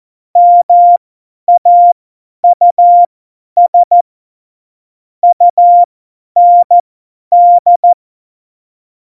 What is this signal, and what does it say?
Morse code, heard as a keyed tone, about 14 wpm, spaced as for 10 wpm, 700 Hz. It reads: MAUS UND
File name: Morsekurs / Witze / Witz 11. Dezember, 50 F70